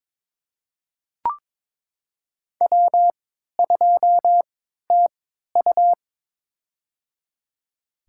Morse code, heard as a keyed tone, about 21 wpm, spaced as for 12 wpm, 700 Hz. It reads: W2TU
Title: USA Call Signs - 3x Character Spacing - 22wpm